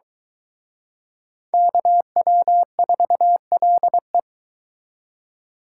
Morse code, heard as a keyed tone, about 24 wpm, 700 Hz. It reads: KW4LE